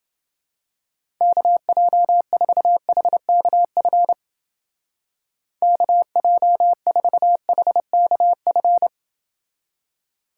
Morse code, heard as a keyed tone, about 28 wpm, 700 Hz. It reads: KJ4HKF KJ4HKF